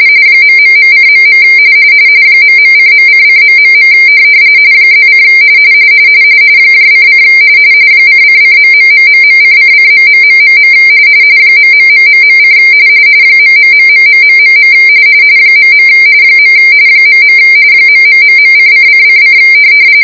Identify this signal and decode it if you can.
RTTY, about 45.45 baud. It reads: ...:.. ..:.::... +: :: :::.....:::..... .....: ........ .. .. ....:::::